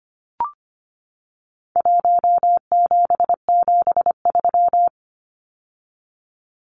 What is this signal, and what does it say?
Morse code, heard as a keyed tone, about 25 wpm, 700 Hz. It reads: 1773
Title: Numbers - Contest Style - 4-Digit 25wpm